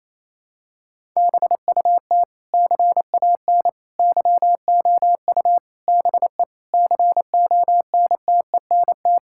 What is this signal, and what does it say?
Morse code, heard as a keyed tone, about 27 wpm, 700 Hz. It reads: BUT CAN YOU BE CONTENT
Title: Sentences from Top 1000 Words - Review Part 6 - 28wpm